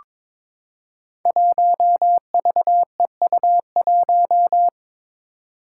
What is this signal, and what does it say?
Morse code, heard as a keyed tone, about 21 wpm, 700 Hz. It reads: 1VEU1
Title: Five Challenging Letter-Number Combinations 22wpm